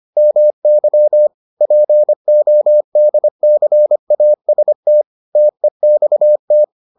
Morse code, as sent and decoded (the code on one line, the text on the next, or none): -- -.-- / .--. --- -.. -.-. .- ... - / - . -..- -
MY PODCAST TEXT